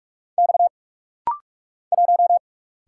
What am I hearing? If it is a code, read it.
Morse code, heard as a keyed tone, about 44 wpm, 700 Hz. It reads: X 1
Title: Single Letter-Number Alt - Rapid-Fire 45wpm